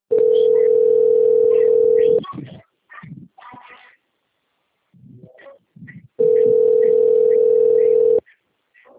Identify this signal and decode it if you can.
Ringback tone (North America), 440 + 480 Hz.
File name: out- - - - - .19045